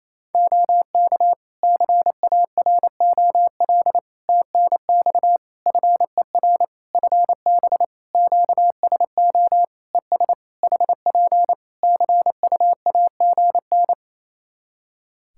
Morse code, heard as a keyed tone, about 28 wpm, 700 Hz. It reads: OK CAROL TNX FER FB QSO ES HP CUAGN
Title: Practice QSO - Exchange 4 - Ending - 28wpm